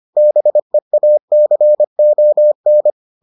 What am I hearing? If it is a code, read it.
Morse code, heard as a keyed tone, about 25 wpm, 600 Hz. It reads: BEACON